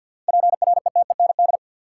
Morse code, heard as a keyed tone, about 50 wpm, 700 Hz. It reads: PRETEND
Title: Seven Letter Words - 50wpm